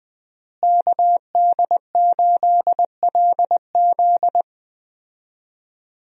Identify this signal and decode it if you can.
Morse code, heard as a keyed tone, about 20 wpm, 700 Hz. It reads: KD8LZ
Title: USA Call Signs - 5 Characters - Repeated 3x - 20wpm